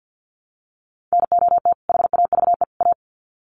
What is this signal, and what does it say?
Morse code, heard as a keyed tone, about 49 wpm, 700 Hz. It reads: NOT HAVE A
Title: Sets of 3 Words 50wpm